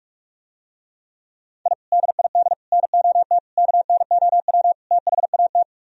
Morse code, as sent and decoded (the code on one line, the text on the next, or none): .. / -.. .. -.. / -. --- - / -.- -. --- .-- / - .... .- -
I DID NOT KNOW THAT